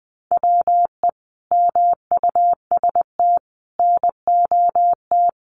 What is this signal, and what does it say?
Morse code, heard as a keyed tone, about 20 wpm, 700 Hz. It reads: WE MUST NOT
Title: Sentences from Top 1300 Words 20wpm